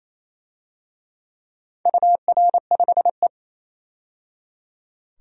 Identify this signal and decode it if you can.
Morse code, heard as a keyed tone, about 28 wpm, 700 Hz. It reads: UR5E
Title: DXCC-100 Call Signs - 4 Characters - 28wpm